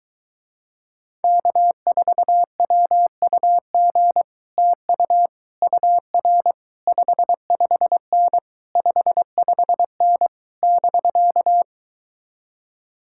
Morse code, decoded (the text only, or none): K4WUG TU UR 55N 55N <BK>